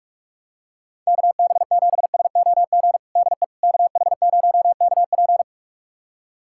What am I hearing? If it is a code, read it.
Morse code, heard as a keyed tone, about 46 wpm, 700 Hz. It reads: KB7SOG DE KH0KP
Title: USA Callsign DE Callsign 45wpm